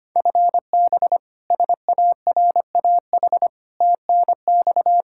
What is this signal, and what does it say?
Morse code, heard as a keyed tone, about 25 wpm, 700 Hz. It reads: FB SARAH TNX
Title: K1USN SST QSO Practice 25wpm